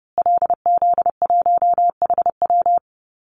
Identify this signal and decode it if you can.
Morse code, heard as a keyed tone, about 30 wpm, 700 Hz. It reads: LZ1HW
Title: Call Signs Repeated 2x - 30wpm